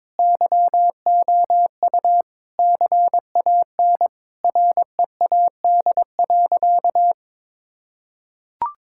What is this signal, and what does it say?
Morse code, heard as a keyed tone, about 22 wpm, 700 Hz. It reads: YOU CAN READ.
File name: practice / 22wpm / Sentences with 8-Word Phrases - 22wpm